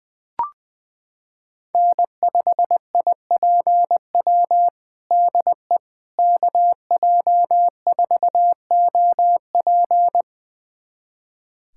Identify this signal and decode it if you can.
Morse code, heard as a keyed tone, about 20 wpm, 700 Hz. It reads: N5IPW DE KJ4OP
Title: USA Callsign DE Callsign - 5 Characters - 20wpm